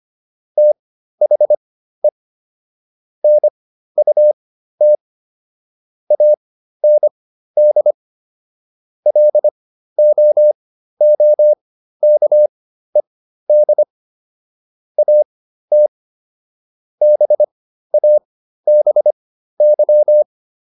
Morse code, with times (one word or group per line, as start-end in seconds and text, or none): 0.6-2.1 THE
3.2-5.0 NUT
6.1-7.9 AND
9.1-13.8 LOOKED
15.0-15.9 AT
17.0-20.2 BABY